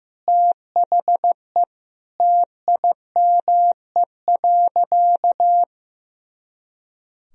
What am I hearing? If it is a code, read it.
Morse code, heard as a keyed tone, about 15 wpm, 700 Hz. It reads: THE TIME.
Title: Sentences from Top 200 Words and Abbreviations 15wpm